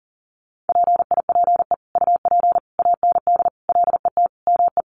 Morse code, heard as a keyed tone, about 37 wpm, 700 Hz. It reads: PIPE UP AND LET ME
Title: Sentences from Top 1500 Words 40wpm